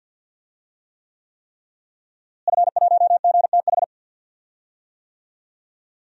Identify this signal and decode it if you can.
Morse code, heard as a keyed tone, about 46 wpm, 700 Hz. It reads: F1GTH
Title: DXCC-100 Call Signs - 5 Characters - Repeated 3x - 50wpm